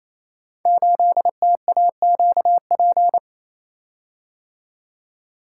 Morse code, heard as a keyed tone, about 28 wpm, 700 Hz. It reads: 8TAQP
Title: Code Groups - 5 Letter:Numbers - 28wpm